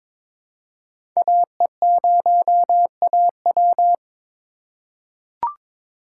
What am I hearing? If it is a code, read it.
Morse code, heard as a keyed tone, about 22 wpm, 700 Hz. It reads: AE0AW
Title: USA Call Signs - 5 Characters - 3x Character Spacing - 22wpm